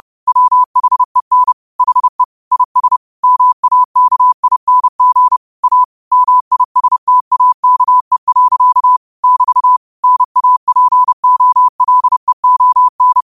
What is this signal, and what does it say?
Morse code, heard as a keyed tone, about 30 wpm, 1000 Hz. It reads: WHEN HE IS MAKING A MISTAKE. <BT> NAPOLEON